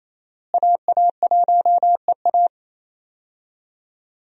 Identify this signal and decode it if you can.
Morse code, heard as a keyed tone, about 28 wpm, 700 Hz. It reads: AA1EA
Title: USA Call Signs - 5 Characters - Repeated 3x - 28wpm